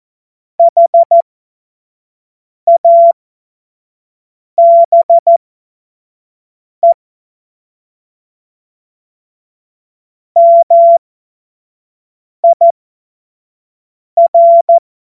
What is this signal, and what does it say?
Morse code, heard as a keyed tone, about 14 wpm, spaced as for 5 wpm, 680 Hz. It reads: HABE MIR
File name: Morsekurs / Witze / Witz 12 März 2024 25 F70